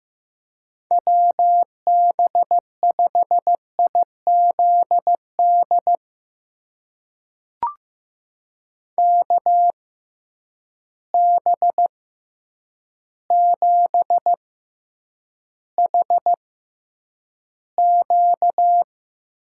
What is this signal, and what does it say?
Morse code, heard as a keyed tone, about 15 wpm, spaced as for 8 wpm, 700 Hz. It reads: WB5IZD KB7HQ
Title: USA Call Signs - 6 Characters - 6x Character Spacing - 15wpm